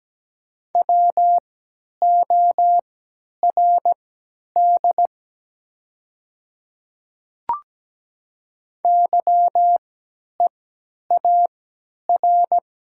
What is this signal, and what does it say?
Morse code, heard as a keyed tone, about 17 wpm, spaced as for 10 wpm, 700 Hz. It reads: WORD YEAR
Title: 3x Character Spacing - Top 200 Words - 17wpm